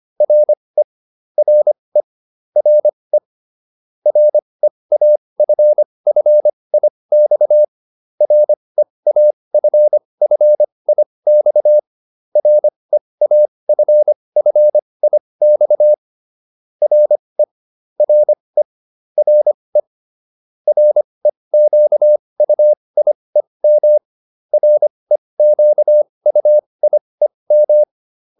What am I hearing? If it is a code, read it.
Morse code, heard as a keyed tone, about 25 wpm, spaced as for 20 wpm, 600 Hz. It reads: RE RE RE REAFFIX REAFFIX REAFFIX RE RE RE REQUIEM REQUIEM